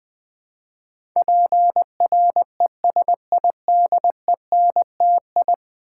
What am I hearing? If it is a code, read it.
Morse code, heard as a keyed tone, about 20 wpm, 700 Hz. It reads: PRESIDENTI
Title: Build A Word - 5 Letters At A Time - 12-Letter Words - 20wpm